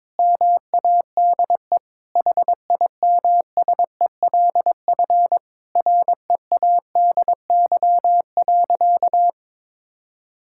Morse code, heard as a keyed tone, about 22 wpm, 700 Hz. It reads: MADE HIMSELF READY.
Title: Sentences from Top 1600 Words 22wpm